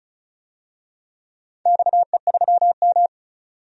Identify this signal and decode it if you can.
Morse code, heard as a keyed tone, about 36 wpm, 700 Hz. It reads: XE3M
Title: DXCC-100 Call Signs - 4 Characters - 35wpm